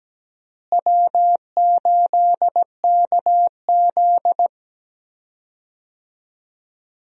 Morse code, heard as a keyed tone, about 17 wpm, 700 Hz. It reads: W8KZ
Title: USA Call Signs - 4 Characters - Repeated 2x - 17wpm